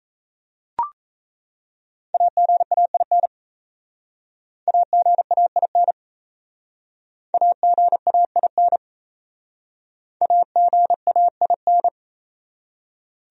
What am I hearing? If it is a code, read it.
Morse code, heard as a keyed tone, about 37 wpm, 700 Hz. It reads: AGAIN AGAIN AGAIN AGAIN